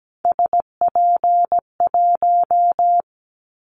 Morse code, heard as a keyed tone, about 17 wpm, 700 Hz. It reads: SP1
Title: Callsign Prefixes for Top 25 DXCC - Repeated 3x - 17wpm